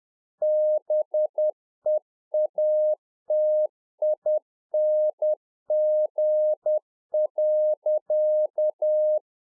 Morse code, heard as a keyed tone, about 10 wpm, 610 Hz. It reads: BEATING.
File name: UNFOGETTABLE QSO 10WPM